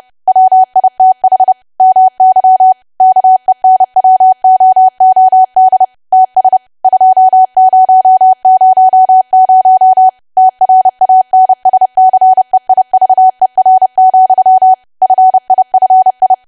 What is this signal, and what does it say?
Morse code, heard as a keyed tone, about 30 wpm, 750 Hz. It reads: WITH MY KENWOOD TS 2000 TRANSCEIVER, FIFI
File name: cwa / W1AW-Level3 / 30WPM